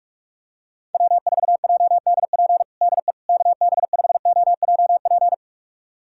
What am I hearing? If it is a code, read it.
Morse code, heard as a keyed tone, about 46 wpm, 700 Hz. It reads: W4JDP DE KB5OJP
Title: USA Callsign DE Callsign 45wpm